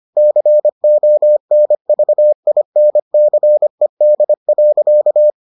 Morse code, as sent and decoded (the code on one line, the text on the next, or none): -.-. --- -. ...- .. -. -.-. -.. .-.-.-
CONVINCED.